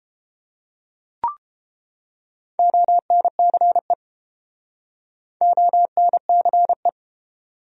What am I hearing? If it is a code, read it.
Morse code, heard as a keyed tone, about 30 wpm, 700 Hz. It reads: ONCE ONCE